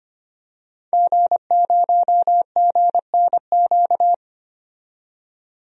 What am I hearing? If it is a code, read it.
Morse code, heard as a keyed tone, about 25 wpm, 700 Hz. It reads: G0GNQ
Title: DXCC-100 Call Signs - 5 Characters - 25wpm